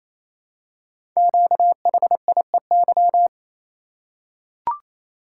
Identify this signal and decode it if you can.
Morse code, heard as a keyed tone, about 27 wpm, 700 Hz. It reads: QHIEY
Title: Code Groups - 5 Letters - 28wpm